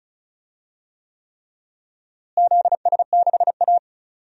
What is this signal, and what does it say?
Morse code, heard as a keyed tone, about 36 wpm, 700 Hz. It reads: ZS6A
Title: DXCC-100 Call Signs - 4 Characters - Repeated 3x - 35wpm